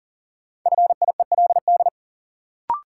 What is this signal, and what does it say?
Morse code, heard as a keyed tone, about 39 wpm, 700 Hz. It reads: FIELD
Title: Sentences from Top 1000 Words - New Word Drill - 40wpm